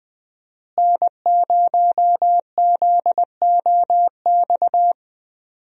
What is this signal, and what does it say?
Morse code, heard as a keyed tone, about 20 wpm, 700 Hz. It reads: N0ZOX
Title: USA Call Signs - 5 Characters - Repeated 2x - 20wpm